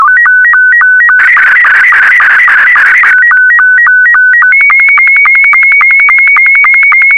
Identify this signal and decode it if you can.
SSTV (slow-scan television), Scottie 2.